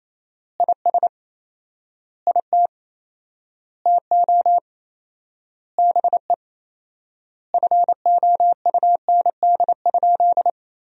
Morse code, heard as a keyed tone, about 28 wpm, 700 Hz. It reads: IS IT TO BE FOUND?
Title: 4x Word-Spacing - Sentences from Top 300 Words - 28wpm